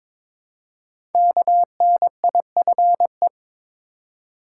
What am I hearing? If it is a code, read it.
Morse code, heard as a keyed tone, about 22 wpm, 700 Hz. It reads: KNIFE
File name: 5 Letter Words - First 3 Letters Given - 22wpm